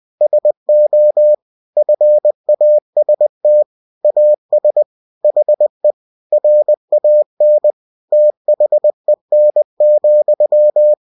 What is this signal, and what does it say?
Morse code, heard as a keyed tone, about 20 wpm, 600 Hz. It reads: SO FAST AS HE RAN THEN,